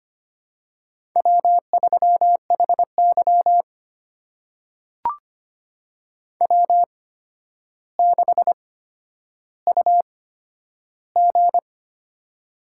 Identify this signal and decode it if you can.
Morse code, heard as a keyed tone, about 25 wpm, spaced as for 11 wpm, 700 Hz. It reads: W3HY W6UG
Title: USA Call Signs - 4 Characters - 8x Character Spacing - 25wpm